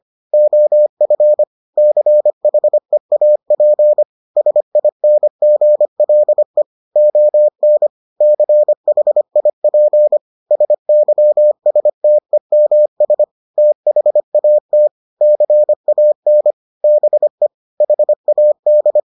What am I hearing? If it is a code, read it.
Morse code, heard as a keyed tone, about 25 wpm, 600 Hz. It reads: OF CHEAP SINGLE ON CHIP SYSTEMS THAT CAN BE HAD